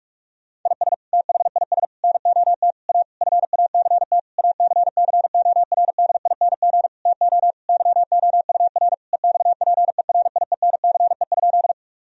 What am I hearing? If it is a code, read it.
Morse code, heard as a keyed tone, about 42 wpm, 700 Hz. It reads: IS THIS NOT A FACT ACCORDING TO YOUR EXPERIENCE?
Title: Sentences from Top 1000 Words - Review Part 4 - 45wpm